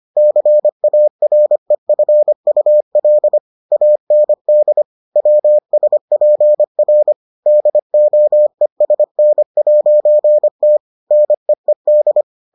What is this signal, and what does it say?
Morse code, heard as a keyed tone, about 25 wpm, 600 Hz. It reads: CAREFUL AND WSPR DOESN'T NEED